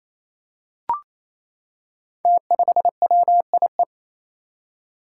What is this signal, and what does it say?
Morse code, heard as a keyed tone, about 28 wpm, 700 Hz. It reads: T5WIE
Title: Five Challenging Letter-Number Combinations 28wpm